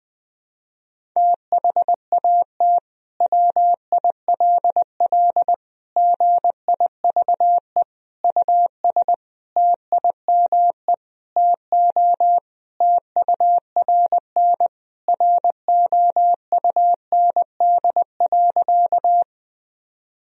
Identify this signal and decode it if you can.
Morse code, heard as a keyed tone, about 20 wpm, 700 Hz. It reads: THAT WILL GIVE US TIME TO TURN ROUND.
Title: Sentences from Top 1000 Words - Review Part 1 - 20wpm